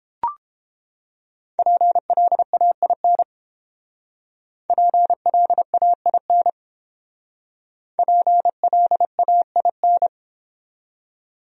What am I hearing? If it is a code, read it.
Morse code, heard as a keyed tone, about 29 wpm, 700 Hz. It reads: PLAIN PLAIN PLAIN